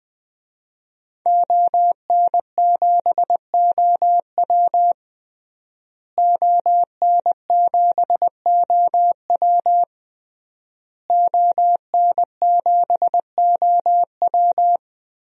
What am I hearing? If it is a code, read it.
Morse code, heard as a keyed tone, about 20 wpm, 700 Hz. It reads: ON7OW ON7OW ON7OW
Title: DXCC-100 Call Signs - 5 Characters - Repeated 3x - 20wpm